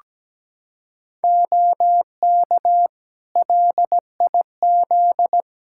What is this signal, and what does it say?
Morse code, heard as a keyed tone, about 17 wpm, 700 Hz. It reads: OK LIZ
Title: K1USN SST QSO Practice 17wpm